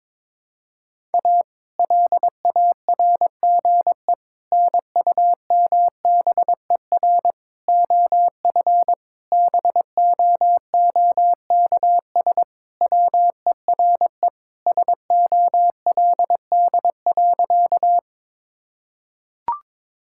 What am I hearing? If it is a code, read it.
Morse code, heard as a keyed tone, about 22 wpm, 700 Hz. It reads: A LARGE NUMBER OF BOOKS WERE SOLD.
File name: practice / 22wpm / Sentences with 4-Word Phrases - 22wpm